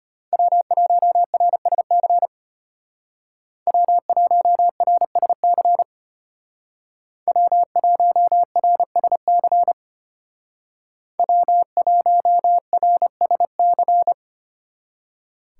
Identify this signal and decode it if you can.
Morse code, heard as a keyed tone, about 32 wpm, 700 Hz. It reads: W1RSC W1RSC W1RSC W1RSC